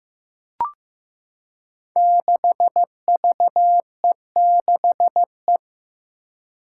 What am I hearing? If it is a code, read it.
Morse code, heard as a keyed tone, about 15 wpm, 700 Hz. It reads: 6VE6E
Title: Five Challenging Letter-Number Combinations 15wpm